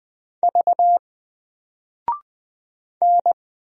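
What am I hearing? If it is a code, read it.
Morse code, heard as a keyed tone, about 20 wpm, 700 Hz. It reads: V N
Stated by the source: Single letter and number Morse drill at 20 wpm, from N4LCD.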